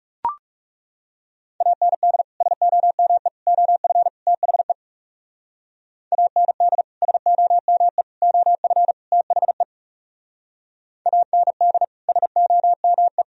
Morse code, heard as a keyed tone, about 37 wpm, 700 Hz. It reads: AND SOME OF THE AND SOME OF THE AND SOME
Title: Speed-Racing - Sets of 4 Words 30wpm